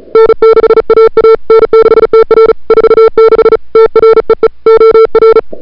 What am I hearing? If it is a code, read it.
Morse code, heard as a keyed tone, about 36 wpm, 440 Hz. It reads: N6AA N6TR 46 TREE OR
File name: n6tr / Fab2007CWSprintHour1